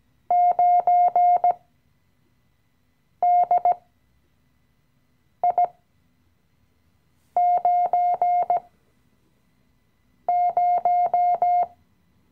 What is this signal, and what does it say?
Morse code, heard as a keyed tone, about 17 wpm, spaced as for 5 wpm, 710 Hz. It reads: 9DI90
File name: CW-Online-Kurs / Anfänger / Lektion 018